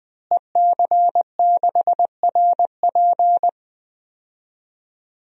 Morse code, heard as a keyed tone, about 20 wpm, 700 Hz. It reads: EC6RP
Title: DXCC-100 Call Signs - 5 Characters - Repeated 3x - 20wpm